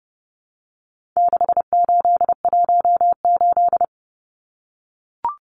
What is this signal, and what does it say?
Morse code, heard as a keyed tone, about 30 wpm, 700 Hz. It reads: 6818
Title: Numbers - 4-Digit 30wpm